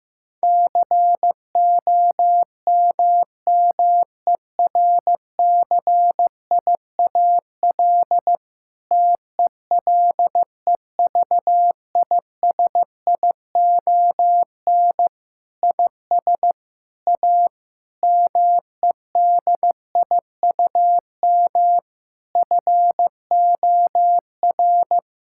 Morse code, heard as a keyed tone, about 15 wpm, 700 Hz. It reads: COMMERCIAL TELEVISION IS A MEDIUM FOR
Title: Sentences from Top 1000 Words - Review Part 6 - 15wpm